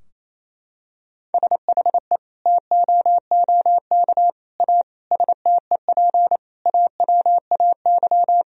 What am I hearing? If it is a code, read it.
Morse code, heard as a keyed tone, about 28 wpm, 700 Hz. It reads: SHE TOOK A STEP AWAY
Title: Sentences from Top 1000 Words - Review Part 2 - 28wpm